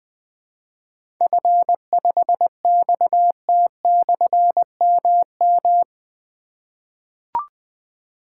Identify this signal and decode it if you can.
Morse code, heard as a keyed tone, about 20 wpm, 700 Hz. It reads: F5XT/MM
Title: Difficult Call Signs 20wpm